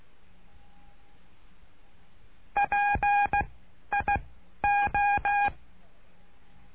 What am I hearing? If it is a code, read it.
Morse code, heard as a keyed tone, about 16 wpm, spaced as for 11 wpm, 810 Hz. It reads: PIO
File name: cursos / CURSO DE TELEGRAFIA / Telegrafia - Trilha 09